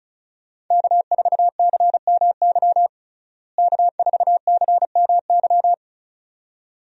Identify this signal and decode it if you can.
Morse code, heard as a keyed tone, about 36 wpm, 700 Hz. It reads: K4CMY K4CMY
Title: USA Call Signs - 5 Characters - Repeated 2x - 35wpm